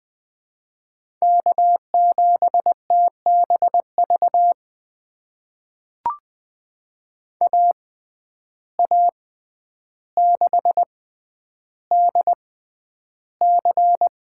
Morse code, heard as a keyed tone, about 20 wpm, spaced as for 10 wpm, 700 Hz. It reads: K7TBV AA6DC
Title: USA Call Signs - 5 Characters - 6x Character Spacing - 20wpm